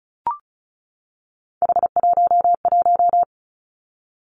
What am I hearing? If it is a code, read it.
Morse code, heard as a keyed tone, about 36 wpm, 700 Hz. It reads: H11
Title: Three Challenging Letter-Number Combinations 35wpm